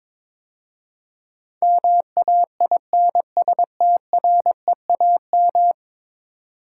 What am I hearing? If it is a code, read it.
Morse code, heard as a keyed tone, about 22 wpm, 700 Hz. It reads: MAINSTREAM
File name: practice / 22wpm / 10 Letter Words - First 3 Letters Given - 22wpm